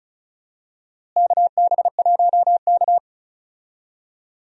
Morse code, heard as a keyed tone, about 36 wpm, 700 Hz. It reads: KB1K
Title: USA Call Signs - 4 Characters - Repeated 3x - 35wpm